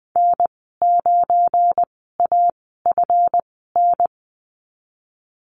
Morse code, heard as a keyed tone, about 20 wpm, spaced as for 15 wpm, 700 Hz. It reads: N9AFN
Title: 173 Lesson 28g - Number 9 - Call Signs 20wpm